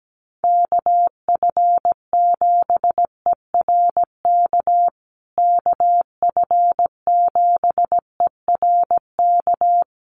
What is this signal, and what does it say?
Morse code, heard as a keyed tone, about 17 wpm, 700 Hz. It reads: KF7ERK KF7ERK
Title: USA Call Signs Repeated 3x 17wpm